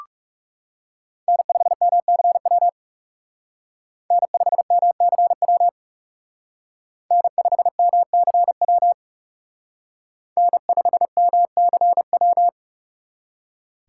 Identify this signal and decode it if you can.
Morse code, heard as a keyed tone, about 36 wpm, 700 Hz. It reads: N5MCW N5MCW N5MCW N5MCW
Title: Speed Racing - USA Call Signs 30wpm